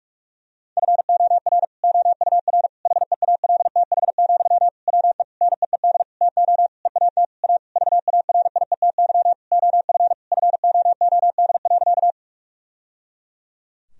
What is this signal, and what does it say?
Morse code, heard as a keyed tone, about 42 wpm, 700 Hz. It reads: FOR OUR HEALTH, WE NEED TO EAT A VARIETY OF FOOD.
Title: Sentences from Top 1000 Words - Review Part 4 - 45wpm